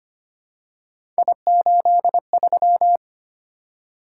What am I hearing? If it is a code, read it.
Morse code, heard as a keyed tone, about 25 wpm, 700 Hz. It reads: I83
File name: Three Letter-Number Combinations - 25wpm